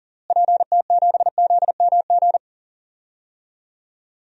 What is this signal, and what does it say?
Morse code, heard as a keyed tone, about 40 wpm, 700 Hz. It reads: PT7ZMG
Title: DXCC-100 Call Signs - 6 Characters - Repeated 2x - 40wpm